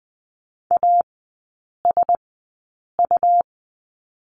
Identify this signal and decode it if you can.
Morse code, heard as a keyed tone, about 20 wpm, spaced as for 8 wpm, 700 Hz. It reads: ASU
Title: 114 Lesson 19e - Letter w - Three Characters 20wpm